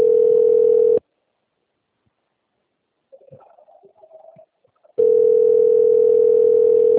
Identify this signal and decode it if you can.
Ringback tone (North America), 440 + 480 Hz.